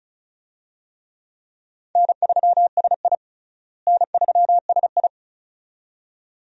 Morse code, heard as a keyed tone, about 36 wpm, 700 Hz. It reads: N3SI N3SI
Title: USA Call Signs - 4 Characters - Repeated 2x - 35wpm